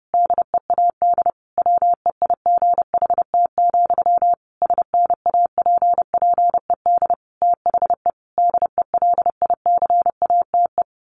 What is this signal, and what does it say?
Morse code, heard as a keyed tone, about 30 wpm, 700 Hz. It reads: DEAD WEIGHT, SNAPPED THE DELICATE
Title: A Sound Of Thunder 30wpm